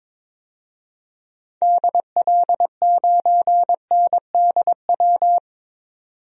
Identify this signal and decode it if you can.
Morse code, heard as a keyed tone, about 23 wpm, 700 Hz. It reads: DL9NDW
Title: DXCC-100 Call Signs - 6 Characters - 22wpm